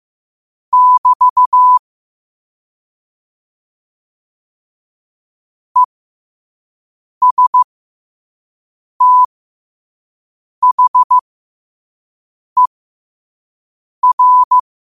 Morse code, heard as a keyed tone, about 15 wpm, spaced as for 5 wpm, 1000 Hz. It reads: <BT> ESTHER
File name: 5wpm